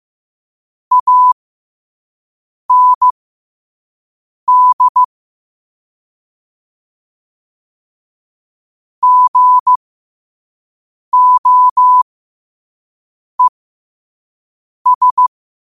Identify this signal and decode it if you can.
Morse code, heard as a keyed tone, about 15 wpm, spaced as for 5 wpm, 1000 Hz. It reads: AND GOES
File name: qotdfiles / 5wpm